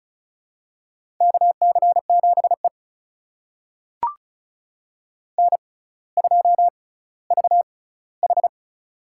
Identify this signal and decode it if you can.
Morse code, heard as a keyed tone, about 36 wpm, spaced as for 18 wpm, 700 Hz. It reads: KC7E N2VH
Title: USA Call Signs - 4 Characters - 6x Character Spacing - 35wpm